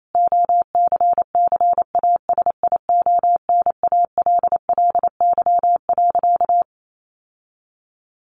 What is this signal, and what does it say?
Morse code, heard as a keyed tone, about 28 wpm, 700 Hz. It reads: OCCASIONALLY.
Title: Sentences from Top 1300 Words 28wpm